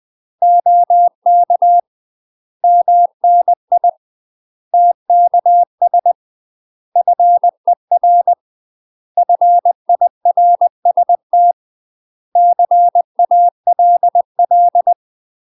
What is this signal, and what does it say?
Morse code, heard as a keyed tone, about 20 wpm, 700 Hz. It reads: OK MNI TKS FER FIRST CALL
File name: UK MASTER QSOs TRAINING / 20WPM / UK MASTER QSOs 0059